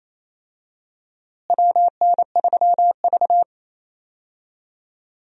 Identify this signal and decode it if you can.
Morse code, heard as a keyed tone, about 28 wpm, 700 Hz. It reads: WN3V
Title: USA Call Signs - 4 Characters - Repeated 3x - 28wpm